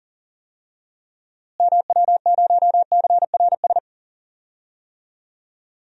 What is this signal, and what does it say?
Morse code, heard as a keyed tone, about 40 wpm, 700 Hz. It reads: MW0CRS
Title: DXCC-100 Call Signs - 6 Characters - Repeated 2x - 40wpm